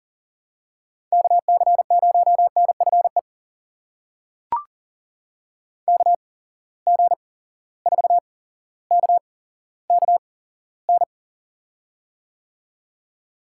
Morse code, heard as a keyed tone, about 40 wpm, spaced as for 17 wpm, 700 Hz. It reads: KC0NFE KG4KKN
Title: USA Call Signs - 6 Characters - 8x Character Spacing - 40wpm